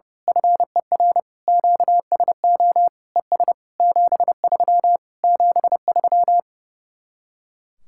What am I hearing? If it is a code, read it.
Morse code, heard as a keyed tone, about 28 wpm, 700 Hz. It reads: FER QSO ES 73 73
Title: Speed-Racing Common QSO Phrases Encore - 30wpm